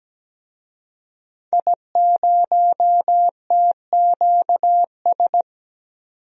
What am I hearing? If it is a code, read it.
Morse code, heard as a keyed tone, about 17 wpm, 700 Hz. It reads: I0TQS
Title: DXCC-100 Call Signs - 5 Characters - 17wpm